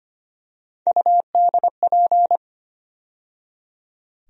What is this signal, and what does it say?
Morse code, heard as a keyed tone, about 25 wpm, 700 Hz. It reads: UDP
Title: USA Call Sign Postfixes - 3 Letters 25wpm